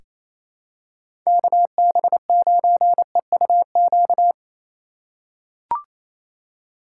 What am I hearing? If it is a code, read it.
Morse code, heard as a keyed tone, about 28 wpm, 700 Hz. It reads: KB9EUQ
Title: USA Call Signs - 4x Character Spacing - 28wpm